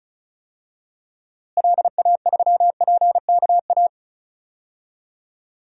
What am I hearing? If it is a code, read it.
Morse code, heard as a keyed tone, about 36 wpm, 700 Hz. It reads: LA3PKA
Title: DXCC-100 Call Signs - 6 Characters - Repeated 2x - 35wpm